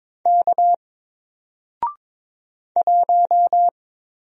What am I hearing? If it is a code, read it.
Morse code, heard as a keyed tone, about 22 wpm, 700 Hz. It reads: K 1
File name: Single-Letter-Number-22wpm